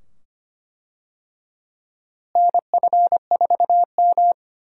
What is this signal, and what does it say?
Morse code, heard as a keyed tone, about 25 wpm, 700 Hz. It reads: NF4M
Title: USA Call Signs - 4 Characters - Repeated 2x - 25wpm